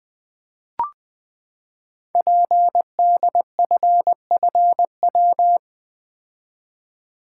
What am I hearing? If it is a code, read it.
Morse code, heard as a keyed tone, about 20 wpm, 700 Hz. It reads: PDFFW
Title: Code Groups - 5 Letter:Numbers - 20wpm